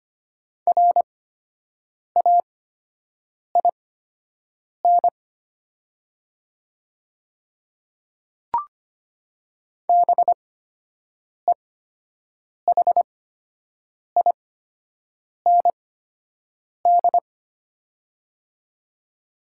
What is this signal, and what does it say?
Morse code, heard as a keyed tone, about 25 wpm, spaced as for 7 wpm, 700 Hz. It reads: RAIN BEHIND
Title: 8x Character Spacing - Top 500 Words - 25wpm